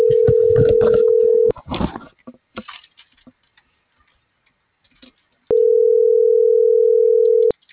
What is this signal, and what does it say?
Ringback tone (North America), 440 + 480 Hz.
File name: CENTRAL AGUILERA